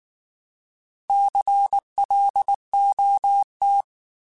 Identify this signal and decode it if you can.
Morse code, heard as a keyed tone, about 19 wpm, 800 Hz. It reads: CLOT
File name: Alphabetical / Alphabetical mp3s / Cs @18